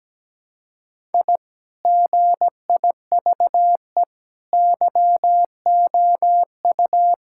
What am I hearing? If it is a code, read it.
Morse code, heard as a keyed tone, about 17 wpm, 700 Hz. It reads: I GIVE YOU
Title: Sentences from Top 1000 Words - Review Part 5 - 17wpm